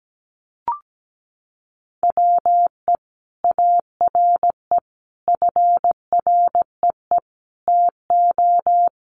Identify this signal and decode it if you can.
Morse code, heard as a keyed tone, about 17 wpm, 700 Hz. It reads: WE ARE FREE TO
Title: Sentences from Top 500 Words 17wpm